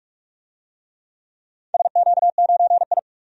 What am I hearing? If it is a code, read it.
Morse code, heard as a keyed tone, about 46 wpm, 700 Hz. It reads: SQ9I
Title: DXCC-100 Call Signs - 4 Characters - 45wpm